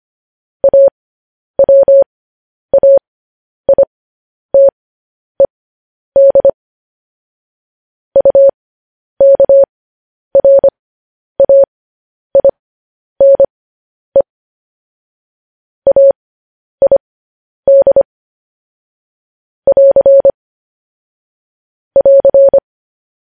Morse code, heard as a keyed tone, about 25 wpm, spaced as for 10 wpm, 550 Hz. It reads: AWAITED UKRAINE AID <AR> <AR>
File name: podcast / News headlines-1x-25x10